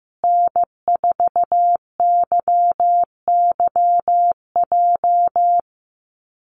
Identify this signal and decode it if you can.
Morse code, heard as a keyed tone, about 15 wpm, 700 Hz. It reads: N4YYJ
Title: USA Call Signs Repeated 3x 15wpm